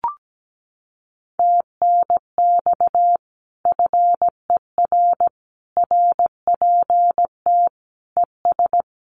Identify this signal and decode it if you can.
Morse code, heard as a keyed tone, about 17 wpm, 700 Hz. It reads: TNX FER RPT ES
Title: Common QSO Phrases Encore 17wpm